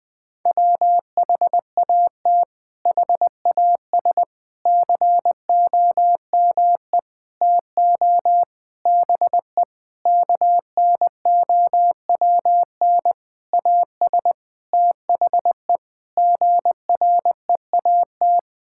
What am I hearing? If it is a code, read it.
Morse code, heard as a keyed tone, about 20 wpm, 700 Hz. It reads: WHAT HAS COME TO BE KNOWN AS THE GREAT
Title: Sentences with 8-Word Phrases - 20wpm